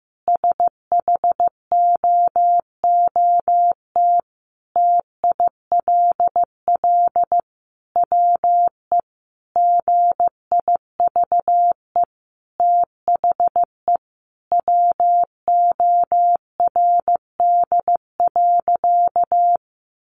Morse code, heard as a keyed tone, about 15 wpm, 700 Hz. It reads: SHOOT TILL WE GIVE THE WORD.
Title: A Sound Of Thunder 15wpm